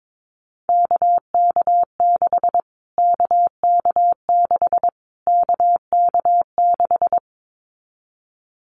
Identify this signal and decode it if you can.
Morse code, heard as a keyed tone, about 22 wpm, 700 Hz. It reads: KK6 KK6 KK6
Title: Callsign Prefixes for Top 25 DXCC - Repeated 3x - 22wpm